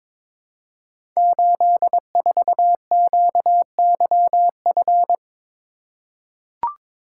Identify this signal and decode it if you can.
Morse code, heard as a keyed tone, about 22 wpm, 700 Hz. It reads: 84QYF
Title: Code Groups - 5 Letter:Numbers - 22wpm